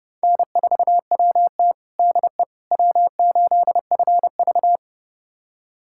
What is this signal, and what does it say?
Morse code, heard as a keyed tone, about 30 wpm, 700 Hz. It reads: N4WT DE W8FV
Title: USA Callsign DE Callsign - 4 Characters - 30wpm